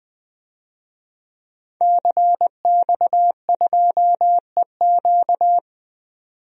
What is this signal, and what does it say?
Morse code, heard as a keyed tone, about 20 wpm, 700 Hz. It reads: CX2EQ
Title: DXCC-100 Call Signs - 5 Characters - 20wpm